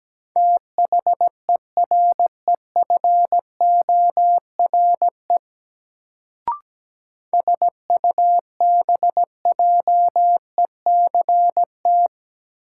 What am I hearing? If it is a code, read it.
Morse code, heard as a keyed tone, about 17 wpm, 700 Hz. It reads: THEREFORE SUBJECT
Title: Sentences from Top 800 Words - New Word Drill - 17wpm